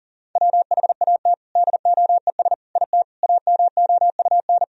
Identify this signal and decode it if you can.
Morse code, heard as a keyed tone, about 40 wpm, 700 Hz. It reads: WHAT DOES IT AMOUN
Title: Sentences from Top 1000 Words - Review Part 4 - 40wpm